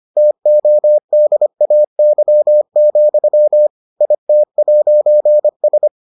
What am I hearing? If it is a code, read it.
Morse code, heard as a keyed tone, about 25 wpm, 600 Hz. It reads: TODAY, IT'S